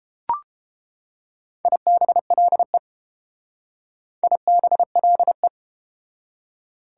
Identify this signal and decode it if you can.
Morse code, heard as a keyed tone, about 31 wpm, 700 Hz. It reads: IBLE IBLE